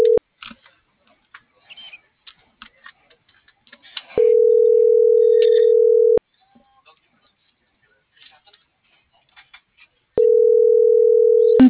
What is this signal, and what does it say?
Ringback tone (North America), 440 + 480 Hz.